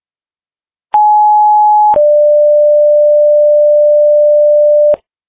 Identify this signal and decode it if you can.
Two-tone page, heard as a one-second tone, then a three-second tone, about 840 then 590 Hz. It is heard from 0.9 to 4.9 s.
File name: Frederika Fire 2026 03 05 19 09 14